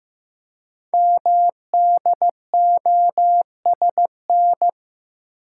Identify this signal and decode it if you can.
Morse code, heard as a keyed tone, about 15 wpm, 700 Hz. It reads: MDOSN
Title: Code Groups - 5 Letter:Numbers - 15wpm